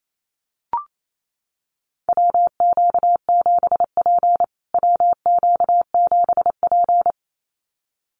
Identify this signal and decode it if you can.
Morse code, heard as a keyed tone, about 28 wpm, 700 Hz. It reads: WQ7P WQ7P
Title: USA Call Signs Repeated 2x 28wpm